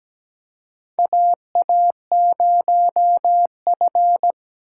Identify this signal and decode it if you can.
Morse code, heard as a keyed tone, about 17 wpm, 700 Hz. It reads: AA0F
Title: USA Call Signs - 4 Characters - 4x Character Spacing - 17wpm